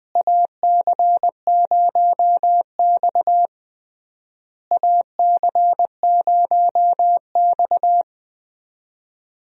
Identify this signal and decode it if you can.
Morse code, heard as a keyed tone, about 20 wpm, 700 Hz. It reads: AC0X AC0X
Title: USA Call Signs - 4 Characters - Repeated 3x - 20wpm